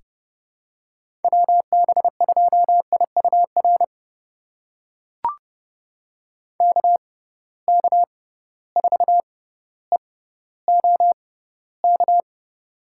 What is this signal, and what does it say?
Morse code, heard as a keyed tone, about 32 wpm, spaced as for 16 wpm, 700 Hz. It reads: WB2IUR KK4EOK
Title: USA Call Signs - 6 Characters - 6x Character Spacing - 30wpm